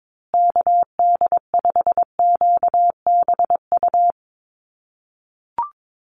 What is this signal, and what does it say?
Morse code, heard as a keyed tone, about 22 wpm, 700 Hz. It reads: KD5QBU
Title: USA Call Signs Repeated 3x 22wpm